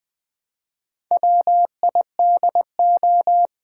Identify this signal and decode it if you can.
Morse code, heard as a keyed tone, about 20 wpm, 700 Hz. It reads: WIDO
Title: Build A Word - 1 Letter At A Time - 5-Letter Words - 20wpm